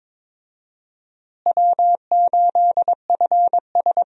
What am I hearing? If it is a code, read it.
Morse code, heard as a keyed tone, about 22 wpm, 700 Hz. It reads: W8FS
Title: USA Call Signs - 4 Characters - Repeated 3x - 22wpm